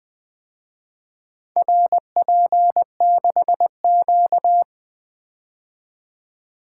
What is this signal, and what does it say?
Morse code, heard as a keyed tone, about 24 wpm, 700 Hz. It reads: RP6Q
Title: Speed-Racing - Four Letter-Number Combinations - 20wpm